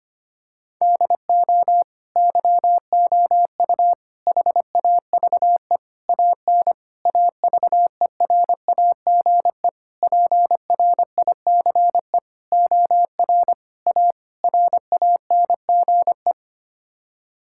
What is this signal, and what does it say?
Morse code, heard as a keyed tone, about 25 wpm, 700 Hz. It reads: DO YOU HAVE AN AVERAGE PRICE OR A RANGE
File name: Sets of 9 Words - 25wpm